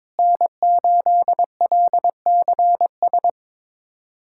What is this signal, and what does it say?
Morse code, heard as a keyed tone, about 22 wpm, 700 Hz. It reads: N8LCS
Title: USA Callsign DE Callsign - 5 Characters - 22wpm